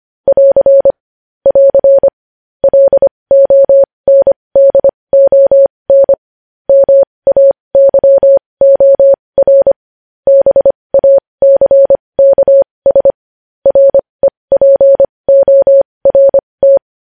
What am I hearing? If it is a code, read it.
Morse code, heard as a keyed tone, about 25 wpm, spaced as for 20 wpm, 550 Hz. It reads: <AR> <AR> LONDON MAYOR BACKS REPORT
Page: news-headlines-in-morse-code-at-20-wpm